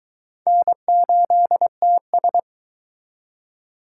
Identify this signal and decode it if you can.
Morse code, heard as a keyed tone, about 24 wpm, 700 Hz. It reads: N8TS